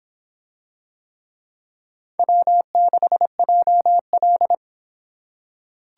Morse code, heard as a keyed tone, about 25 wpm, 700 Hz. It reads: W6JL